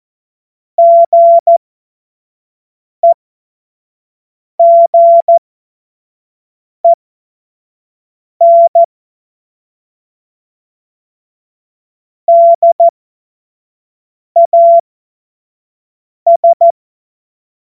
Morse code, heard as a keyed tone, about 14 wpm, spaced as for 5 wpm, 680 Hz. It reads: GEGEN DAS
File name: Morsekurs / Witze / Witz 13. Juli 2021, 25 F70